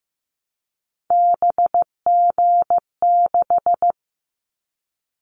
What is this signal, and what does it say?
Morse code, heard as a keyed tone, about 15 wpm, 700 Hz. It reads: BG6
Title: Callsign Prefixes for Top 100 DXCC - 15wpm